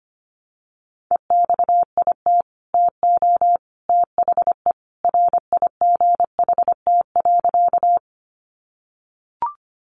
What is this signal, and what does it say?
Morse code, heard as a keyed tone, about 25 wpm, 700 Hz. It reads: EXIT TO THE RIGHT.
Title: Sentences from Top 1100 Words 25wpm